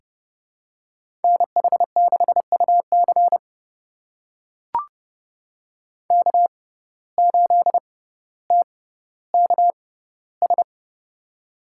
Morse code, heard as a keyed tone, about 29 wpm, spaced as for 15 wpm, 700 Hz. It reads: NH6UC K8TKS
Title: USA Call Signs - 6x Character Spacing - 30wpm